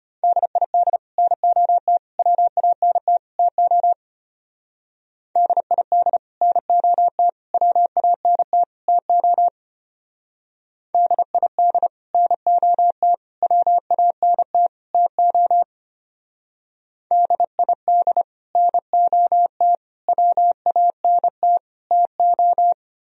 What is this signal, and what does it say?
Morse code, heard as a keyed tone, about 29 wpm, 700 Hz. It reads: DID NOT WANT TO DID NOT WANT TO DID NOT WANT TO DID NOT WANT TO
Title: Speed-Racing - Sets of 4 Words 25wpm